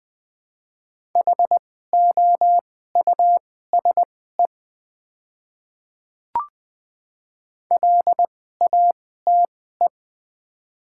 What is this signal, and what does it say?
Morse code, heard as a keyed tone, about 20 wpm, spaced as for 14 wpm, 700 Hz. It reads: HOUSE LATE
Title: 2x Character Spacing - Top 500 Words - 20wpm